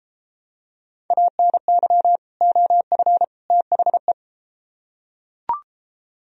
Morse code, heard as a keyed tone, about 33 wpm, 700 Hz. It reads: ANY OF THE